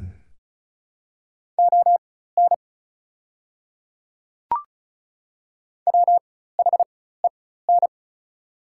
Morse code, heard as a keyed tone, about 36 wpm, spaced as for 16 wpm, 700 Hz. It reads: ON WHEN
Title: 4x Character Spacing - Top 100 Words - 35wpm